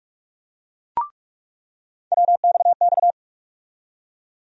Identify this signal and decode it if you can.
Morse code, heard as a keyed tone, about 46 wpm, 700 Hz. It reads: WXX
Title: USA Call Sign Postfixes - 3 Letters 45wpm